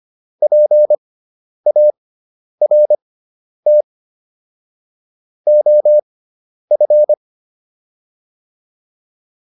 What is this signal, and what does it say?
Morse code, heard as a keyed tone, about 25 wpm, spaced as for 10 wpm, 600 Hz. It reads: PART OF
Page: intermediate-practice-files